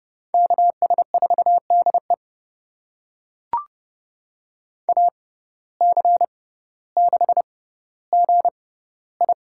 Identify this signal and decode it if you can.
Morse code, heard as a keyed tone, about 30 wpm, spaced as for 15 wpm, 700 Hz. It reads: KS4DE AC6GI
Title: USA Call Signs - 5 Characters - 6x Character Spacing - 30wpm